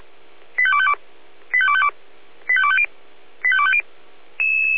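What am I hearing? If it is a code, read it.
Five-tone selective call (ZVEI), heard as quick runs of five tones, tones of about 70 ms. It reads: 85252 85252 85250 85250
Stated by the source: ZVEI five-tone call 85259 received on SDR.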